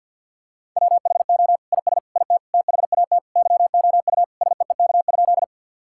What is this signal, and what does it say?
Morse code, heard as a keyed tone, about 50 wpm, 700 Hz. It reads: WHO IS IT THAT YOU SEEK?
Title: Sentences from Top 1600 Words 50wpm